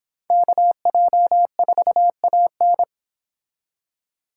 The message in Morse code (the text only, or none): KJ4AN